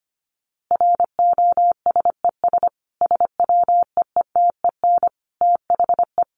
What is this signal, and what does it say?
Morse code, heard as a keyed tone, about 25 wpm, 700 Hz. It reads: ROSES SWEETEN THE
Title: A Sound Of Thunder 25wpm